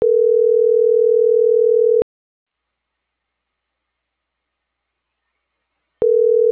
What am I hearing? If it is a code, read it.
Ringback tone (North America), 440 + 480 Hz.